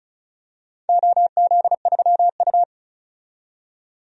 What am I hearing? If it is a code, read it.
Morse code, heard as a keyed tone, about 34 wpm, 700 Hz. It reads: OZ3U
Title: DXCC-100 Call Signs - 4 Characters - 35wpm